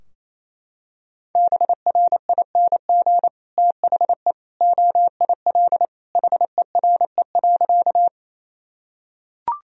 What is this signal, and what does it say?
Morse code, heard as a keyed tone, about 28 wpm, 700 Hz. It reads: BRING THE OIL HERE.
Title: Sentences from Top 1000 Words - Review Part 4 - 28wpm